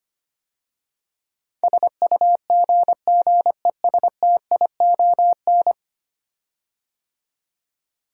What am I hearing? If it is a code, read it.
Morse code, heard as a keyed tone, about 25 wpm, 700 Hz. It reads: SUGGESTION
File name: 10 Letter Words - First 3 Letters Given - 25wpm